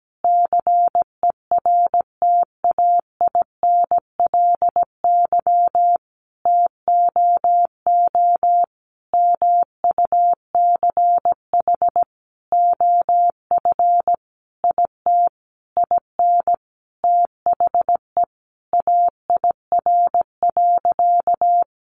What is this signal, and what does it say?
Morse code, heard as a keyed tone, about 17 wpm, 700 Hz. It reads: CERTAINLY TOO MUCH OF IT IN THE AIR.
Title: Sentences from Top 900 Words 17wpm